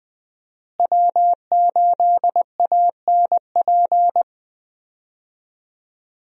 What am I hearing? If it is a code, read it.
Morse code, heard as a keyed tone, about 20 wpm, 700 Hz. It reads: W8ANP
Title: USA Call Signs - 5 Characters - Repeated 2x - 20wpm